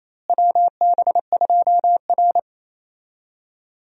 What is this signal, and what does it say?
Morse code, heard as a keyed tone, about 28 wpm, 700 Hz. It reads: WB2R
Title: USA Call Signs - 4 Characters - Repeated 3x - 28wpm